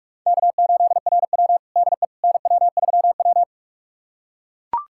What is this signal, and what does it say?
Morse code, heard as a keyed tone, about 42 wpm, 700 Hz. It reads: K8RW DE NW3W
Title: USA Callsign DE Callsign - 4 Characters - 45wpm